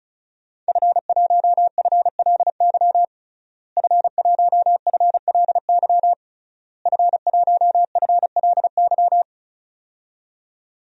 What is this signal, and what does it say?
Morse code, heard as a keyed tone, about 34 wpm, 700 Hz. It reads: F1FLY F1FLY F1FLY
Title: DXCC-100 Call Signs - 5 Characters - Repeated 3x - 35wpm